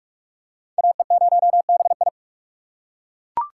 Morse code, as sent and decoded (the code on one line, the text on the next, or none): .- . ----- -... ..
AE0BI